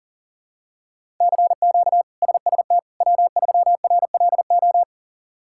Morse code, heard as a keyed tone, about 40 wpm, 700 Hz. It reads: CQ SST W3RLO
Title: K1USN SST QSO Practice - 40wpm